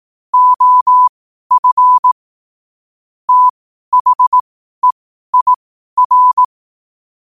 Morse code, heard as a keyed tone, about 18 wpm, spaced as for 12 wpm, 1000 Hz. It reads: OF THEIR
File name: qotdfiles / 12wpm